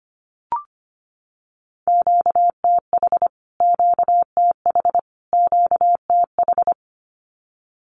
Morse code, heard as a keyed tone, about 25 wpm, 700 Hz. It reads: QTH QTH QTH
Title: Q-Codes Repeated 3x 25wpm